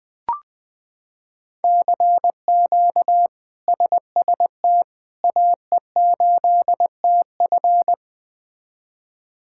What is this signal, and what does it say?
Morse code, heard as a keyed tone, about 20 wpm, 700 Hz. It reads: CQ SST AE8TF
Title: K1USN SST QSO Practice - 20wpm